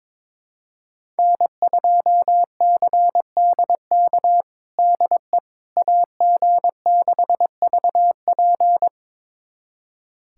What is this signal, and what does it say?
Morse code, heard as a keyed tone, about 22 wpm, 700 Hz. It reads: N2CDK DE AG6VP
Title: USA Callsign DE Callsign - 5 Characters - 22wpm